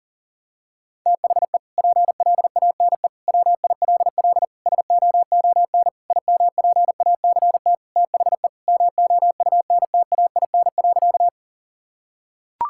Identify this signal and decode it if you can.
Morse code, heard as a keyed tone, about 37 wpm, 700 Hz. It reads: THE PLANE WILL SOON IMPACT THE MOUNTAIN.
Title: Sentences from Top 1000 Words - Review Part 6 - 40wpm